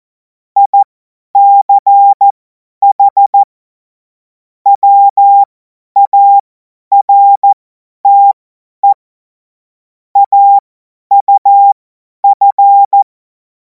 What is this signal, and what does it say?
Morse code, heard as a keyed tone, about 14 wpm, spaced as for 10 wpm, 800 Hz. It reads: ICH WARTE AUF
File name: Morsekurs / Witze / Witz 10. Februar 2026 50 F70